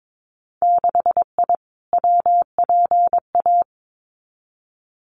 Morse code, heard as a keyed tone, about 22 wpm, 700 Hz. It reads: 6I WPA
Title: Winter Field Day Exchanges Encore 22wpm